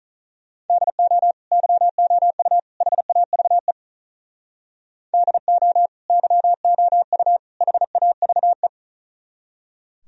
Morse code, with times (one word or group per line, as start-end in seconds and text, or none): 0.7-1.3 DO
1.5-2.6 YOU
2.8-3.7 HAVE
5.1-5.9 DO
6.1-7.4 YOU
7.6-8.7 HAVE